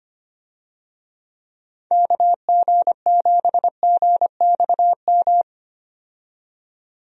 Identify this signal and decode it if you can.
Morse code, heard as a keyed tone, about 25 wpm, 700 Hz. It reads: KG7GXM
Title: USA Call Signs - 6 Characters - 25wpm